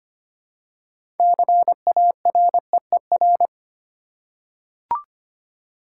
Morse code, heard as a keyed tone, about 25 wpm, 700 Hz. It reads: CAREER
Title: Six Letter Words - 25wpm